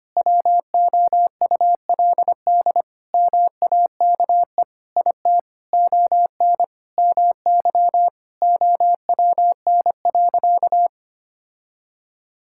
Morse code, heard as a keyed tone, about 25 wpm, 700 Hz. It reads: WOULD MAKE IT ON MY OWN.
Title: Sentences from Top 1000 Words - Review Part 6 - 25wpm